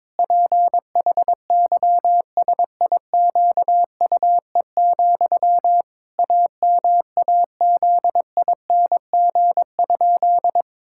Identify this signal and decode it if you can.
Morse code, heard as a keyed tone, about 22 wpm, 700 Hz. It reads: PHYSIQUE, AMAZING?
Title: Pangram Challenge v2 - 22wpm